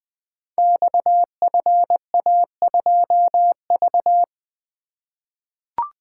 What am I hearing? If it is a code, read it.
Morse code, heard as a keyed tone, about 20 wpm, 700 Hz. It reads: XFA2V
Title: Code Groups - 5 Letter:Numbers - 20wpm